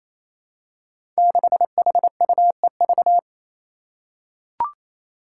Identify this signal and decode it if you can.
Morse code, heard as a keyed tone, about 28 wpm, 700 Hz. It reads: 6HUEV
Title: Five Challenging Letter-Number Combinations 28wpm